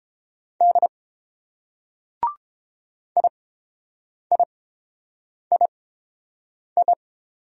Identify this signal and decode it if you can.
Morse code, heard as a keyed tone, about 30 wpm, spaced as for 7 wpm, 700 Hz. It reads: D IIII